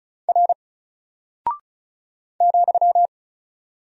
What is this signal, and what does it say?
Morse code, heard as a keyed tone, about 36 wpm, 700 Hz. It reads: R ,
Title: Single Character (Letter:Number:Punctuation:Prosign) 35wpm